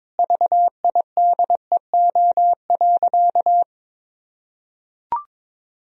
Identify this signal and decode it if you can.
Morse code, heard as a keyed tone, about 22 wpm, 700 Hz. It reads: VIDEO.
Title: Sentences from Top 1000 Words - Review Part 4 - 22wpm